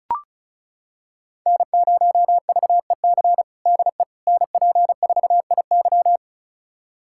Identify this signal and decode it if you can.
Morse code, heard as a keyed tone, about 36 wpm, 700 Hz. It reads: N0VEC DE NP4IY
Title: USA Callsign DE Callsign - 5 Characters - 35wpm